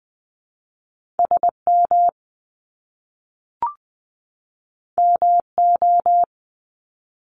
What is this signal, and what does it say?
Morse code, heard as a keyed tone, about 19 wpm, 700 Hz. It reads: SM MO
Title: Top Two Letter Combinations in Words 20wpm